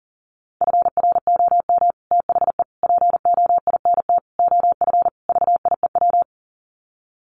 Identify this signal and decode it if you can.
Morse code, heard as a keyed tone, about 40 wpm, 700 Hz. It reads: FROM THE POINT OF VIEW
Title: Sets of 5 Words 40wpm